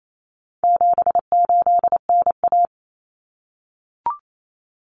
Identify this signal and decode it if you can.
Morse code, heard as a keyed tone, about 28 wpm, 700 Hz. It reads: 78NA
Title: Common Cut Numbers - 4-Digit 28wpm